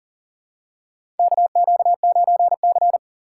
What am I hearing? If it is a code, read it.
Morse code, heard as a keyed tone, about 40 wpm, 700 Hz. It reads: KQ9C
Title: USA Call Signs - 4 Characters - 3x Character Spacing - 40wpm